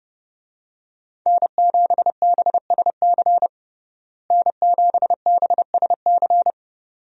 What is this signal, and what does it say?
Morse code, heard as a keyed tone, about 30 wpm, 700 Hz. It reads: N7BSC N7BSC
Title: USA Call Signs - 5 Characters - Repeated 2x - 30wpm